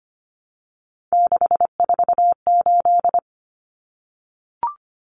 Morse code, heard as a keyed tone, about 25 wpm, 700 Hz. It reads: 648
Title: Numbers - 3-Digit 25wpm